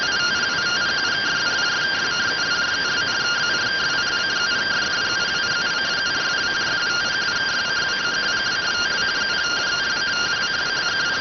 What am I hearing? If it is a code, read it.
Two-tone teleprinter signal, about 170 Hz shift.